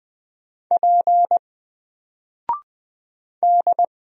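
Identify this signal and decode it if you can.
Morse code, heard as a keyed tone, about 20 wpm, 700 Hz. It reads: P D
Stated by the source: Single letter and number Morse drill at 20 wpm, from N4LCD.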